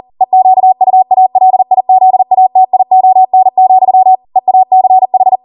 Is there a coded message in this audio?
Yes: Morse code from 0.2 to 5.4 s, heard as a keyed tone, about 40 wpm, 750 Hz. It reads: EQUALIZATION, EACH